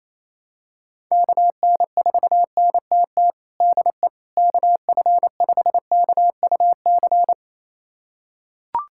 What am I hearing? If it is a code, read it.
Morse code, heard as a keyed tone, about 28 wpm, 700 Hz. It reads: KN4NTT DE KF5KUC
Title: USA Callsign DE Callsign 28wpm